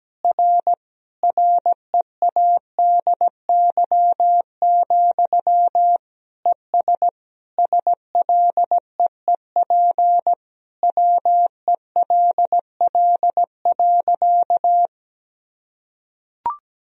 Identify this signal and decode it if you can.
Morse code, heard as a keyed tone, about 17 wpm, 700 Hz. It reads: R READY, ES SLEEP WELL.
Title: Sentences from Top 500 Words and Abbreviations 17wpm